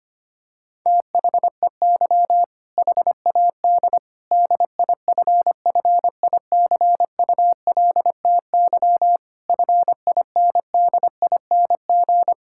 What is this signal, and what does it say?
Morse code, heard as a keyed tone, about 25 wpm, 700 Hz. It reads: THEY HAD DIFFICULTY FINDING
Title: Sentences from Top 1600 Words 25wpm